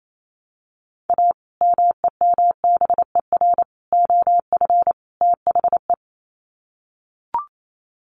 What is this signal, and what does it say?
Morse code, heard as a keyed tone, about 28 wpm, 700 Hz. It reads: A MEMBER OF THE
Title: Sets of 4 Words 28wpm